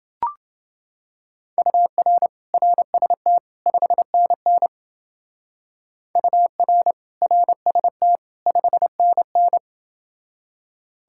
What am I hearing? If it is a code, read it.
Morse code, heard as a keyed tone, about 28 wpm, 700 Hz. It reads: UR RST 5NN UR RST 5NN